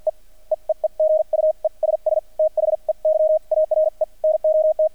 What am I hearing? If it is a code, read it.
Morse code, heard as a keyed tone, about 51 wpm, spaced as for 42 wpm, 630 Hz. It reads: E E E E GUESS THEY ARE NOT
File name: archived / CRSnet